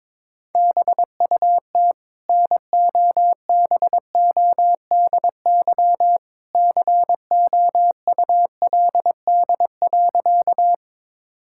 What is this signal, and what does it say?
Morse code, heard as a keyed tone, about 22 wpm, 700 Hz. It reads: BUT NOBODY COULD.
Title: Sentences from Top 1600 Words 22wpm